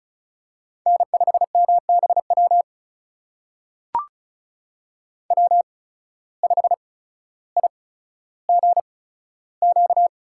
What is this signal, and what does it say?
Morse code, heard as a keyed tone, about 36 wpm, spaced as for 15 wpm, 700 Hz. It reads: N5MBW W5IGQ
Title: USA Call Signs - 5 Characters - 8x Character Spacing - 35wpm